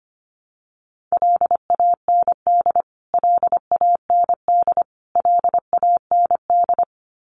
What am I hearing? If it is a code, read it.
Morse code, heard as a keyed tone, about 25 wpm, 700 Hz. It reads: LAND LAND LAND
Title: Top 500 Words Repeated 25wpm